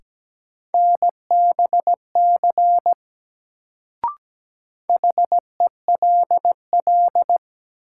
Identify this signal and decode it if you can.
Morse code, heard as a keyed tone, about 17 wpm, 700 Hz. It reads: NBC HELL
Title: Sets of 5 Words Encore - New Word Drill - 17wpm